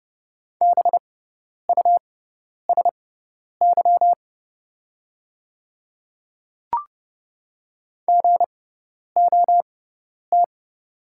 Morse code, heard as a keyed tone, about 29 wpm, spaced as for 10 wpm, 700 Hz. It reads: BUSY GOT
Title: 6x Character Spacing - Top 500 Words - 30wpm